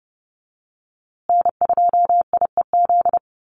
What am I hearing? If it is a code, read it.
Morse code, heard as a keyed tone, about 30 wpm, 700 Hz. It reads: N2IEZ
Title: USA Call Signs Repeated 3x 30wpm